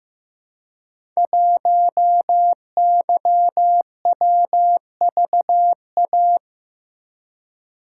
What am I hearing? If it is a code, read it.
Morse code, heard as a keyed tone, about 15 wpm, 700 Hz. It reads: 1YWVA
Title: Code Groups - 5 Letter:Numbers - 15wpm